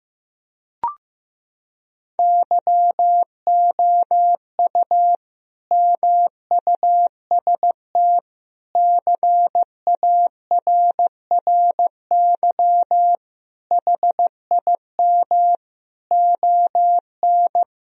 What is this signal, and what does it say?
Morse code, heard as a keyed tone, about 15 wpm, 700 Hz. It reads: YOU MUST CARRY HIM ON
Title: Sentences from Top 1000 Words - Review Part 1 - 15wpm